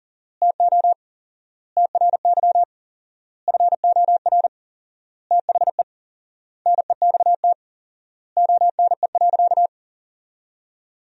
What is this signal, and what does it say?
Morse code, heard as a keyed tone, about 40 wpm, 700 Hz. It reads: TO TRY FOR THE NEXT ONE.
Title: 4x Word-Spacing - Sentences from Top 400 Words - 40wpm